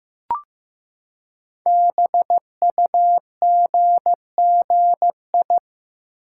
Morse code, heard as a keyed tone, about 15 wpm, 700 Hz. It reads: BUGGI
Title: Code Groups - 5 Letter:Numbers - 15wpm